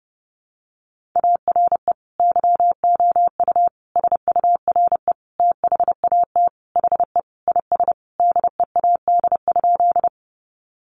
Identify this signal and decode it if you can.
Morse code, heard as a keyed tone, about 30 wpm, 700 Hz. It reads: ARE YOU SURE THAT HE IS DEAD?
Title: Sentences from Top 1500 Words 30wpm